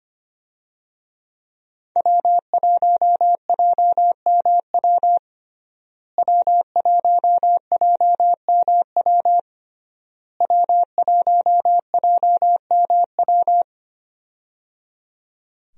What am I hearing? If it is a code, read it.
Morse code, heard as a keyed tone, about 25 wpm, 700 Hz. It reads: W1JMW W1JMW W1JMW
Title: USA Call Signs - 5 Characters - Repeated 3x - 25wpm